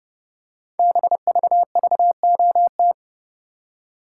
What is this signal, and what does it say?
Morse code, heard as a keyed tone, about 30 wpm, 700 Hz. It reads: BVVOT
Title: Code Groups - 5 Letter:Numbers - 30wpm